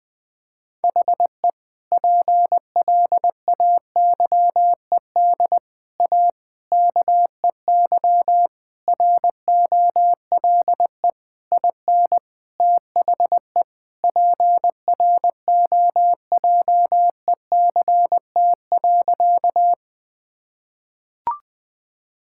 Morse code, heard as a keyed tone, about 20 wpm, 700 Hz. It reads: HE PLAYED A KEY ROLE IN THE PROJECT.
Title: Sentences with 6-Word Phrases - Spoken Beginning - 20wpm